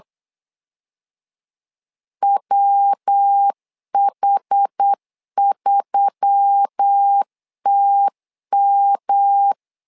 Morse code, heard as a keyed tone, about 8 wpm, 800 Hz. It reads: WH3TM
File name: LEZIONI / 4 Q1S3 8wpm 3037